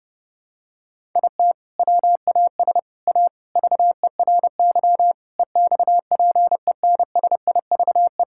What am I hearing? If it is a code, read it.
Morse code, heard as a keyed tone, about 30 wpm, 700 Hz. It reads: IT WAS A VERY EXPENSIVE
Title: Sentences from Top 1000 Words - Review Part 6 - 30wpm